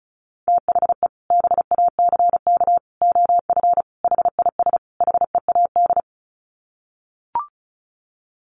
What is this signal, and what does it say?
Morse code, heard as a keyed tone, about 34 wpm, 700 Hz. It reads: THE BACK OF HIS HEAD
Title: Sets of 5 Words Encore 35wpm